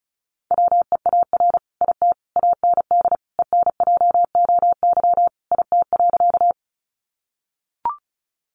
Morse code, heard as a keyed tone, about 34 wpm, 700 Hz. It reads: WEAR IT AND ENJOY IT.
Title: Sentences from Top 1500 Words 35wpm